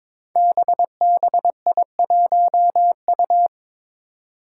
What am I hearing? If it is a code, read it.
Morse code, heard as a keyed tone, about 22 wpm, 700 Hz. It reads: BBI1U
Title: Five Challenging Letter-Number Combinations 22wpm